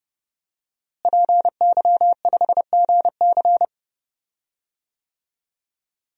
Morse code, heard as a keyed tone, about 32 wpm, 700 Hz. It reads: PY5GC
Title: DXCC-100 Call Signs - 5 Characters - Repeated 2x - 30wpm